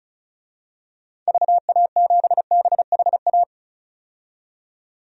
Morse code, heard as a keyed tone, about 36 wpm, 700 Hz. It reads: VA7BHA
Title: DXCC-100 Call Signs - 6 Characters - 35wpm